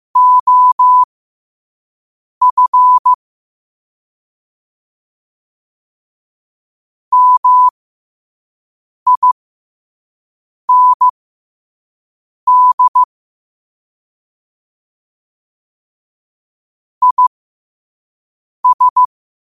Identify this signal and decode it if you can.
Morse code, heard as a keyed tone, about 15 wpm, spaced as for 5 wpm, 1000 Hz. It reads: OF MIND IS
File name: 5wpm